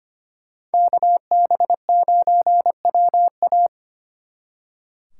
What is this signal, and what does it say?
Morse code, heard as a keyed tone, about 32 wpm, 700 Hz. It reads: KB9WA